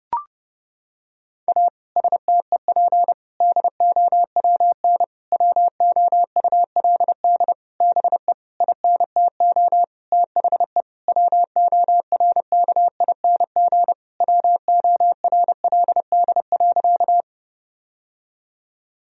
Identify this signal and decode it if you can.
Morse code, heard as a keyed tone, about 30 wpm, 700 Hz. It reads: A STEP DOWN WOULD BE INTO THE WORKING WORLD.
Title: Sentences from Top 1000 Words - Review Part 4 - 30wpm